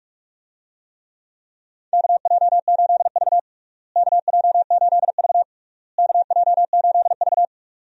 Morse code, heard as a keyed tone, about 46 wpm, 700 Hz. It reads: KJ8V KJ8V KJ8V
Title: USA Call Signs - 4 Characters - Repeated 3x - 45wpm